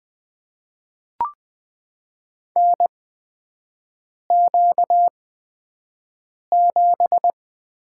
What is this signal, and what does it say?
Morse code, heard as a keyed tone, about 20 wpm, spaced as for 5 wpm, 700 Hz. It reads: NQ7
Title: USA Call Signs - 4 Characters - 8x Character Spacing - 20wpm